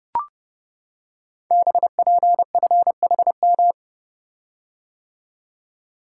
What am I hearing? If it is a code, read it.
Morse code, heard as a keyed tone, about 30 wpm, 700 Hz. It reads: BPFHM
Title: Code Groups - 5 Letters - 30wpm